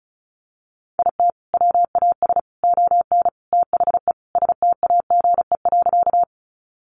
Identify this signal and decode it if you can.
Morse code, heard as a keyed tone, about 34 wpm, 700 Hz. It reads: IT WAS ON THE STAGE.
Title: Sentences from Top 1200 Words 35wpm